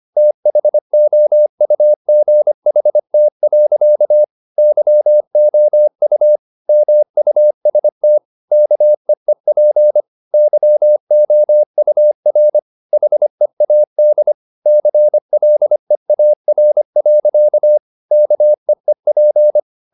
Morse code, heard as a keyed tone, about 25 wpm, 600 Hz. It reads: THOUGHT. YOU MUST KEEP YOUR HEAD CLEAR. KEEP